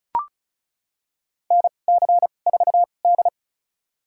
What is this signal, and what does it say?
Morse code, heard as a keyed tone, about 36 wpm, spaced as for 25 wpm, 700 Hz. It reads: NC4D
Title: USA Call Signs - 4 Characters - 2x Character Spacing - 35wpm